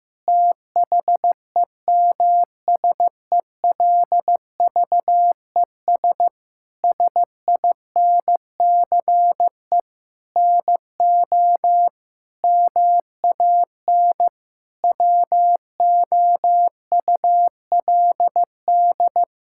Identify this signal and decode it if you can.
Morse code, heard as a keyed tone, about 16 wpm, 700 Hz. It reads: THEMSELVES SINCE NO MAN WOULD
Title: Sentences from Top 1000 Words - Review Part 5 - 15wpm